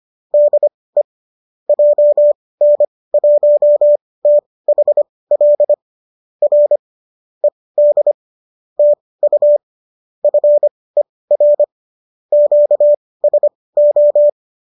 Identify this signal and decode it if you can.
Morse code, heard as a keyed tone, about 25 wpm, spaced as for 18 wpm, 590 Hz. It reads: DE JN1THL R ED TU FER QSO